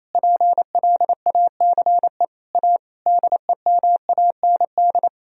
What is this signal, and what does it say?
Morse code, heard as a keyed tone, about 28 wpm, 700 Hz. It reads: PLACE A DEMAND
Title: Sentences from Top 1000 Words - Review Part 4 - 28wpm